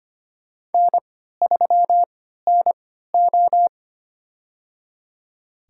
Morse code, heard as a keyed tone, about 25 wpm, spaced as for 14 wpm, 700 Hz. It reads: N3NO
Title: USA Call Signs - 4 Characters - 3x Character Spacing - 25wpm